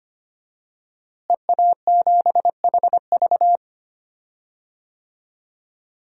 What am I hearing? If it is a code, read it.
Morse code, heard as a keyed tone, about 25 wpm, 700 Hz. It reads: EA7HV
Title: DXCC-100 Call Signs - 5 Characters - Repeated 2x - 25wpm